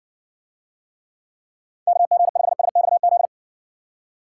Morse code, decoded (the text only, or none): KG5SXZ